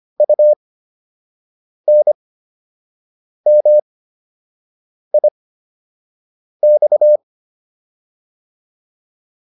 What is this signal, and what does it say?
Morse code, heard as a keyed tone, about 25 wpm, spaced as for 6 wpm, 600 Hz. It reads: UNMIX
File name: UN5 06